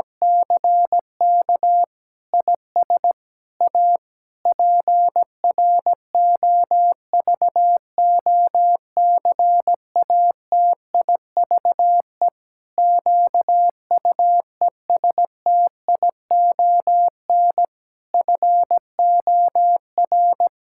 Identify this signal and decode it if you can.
Morse code, heard as a keyed tone, about 17 wpm, 700 Hz. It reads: CK IS A PROVOCATIVE QUESTION FOR
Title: Pangram Challenge v2 - 17wpm